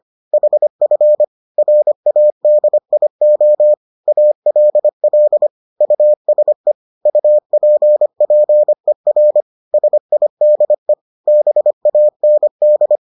Morse code, heard as a keyed tone, about 25 wpm, 600 Hz. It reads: HF RADIO ALL USE UPPER SIDE BAND